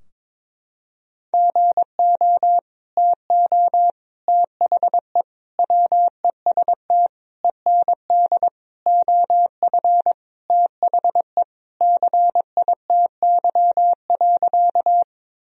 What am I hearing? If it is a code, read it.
Morse code, heard as a keyed tone, about 22 wpm, 700 Hz. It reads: GO TO THE WEST END OF THE CITY.
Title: Sentences from Top 1000 Words - Review Part 2 - 22wpm